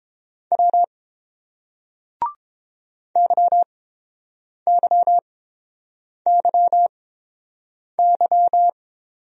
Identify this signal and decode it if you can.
Morse code, heard as a keyed tone, about 30 wpm, spaced as for 7 wpm, 690 Hz. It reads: W YYYY